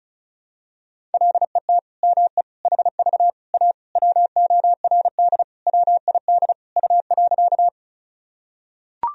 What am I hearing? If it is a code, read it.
Morse code, heard as a keyed tone, about 36 wpm, 700 Hz. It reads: LET ME HV A WORD WID U.
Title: Sentences from Top 300 Words and Abbreviations 35wpm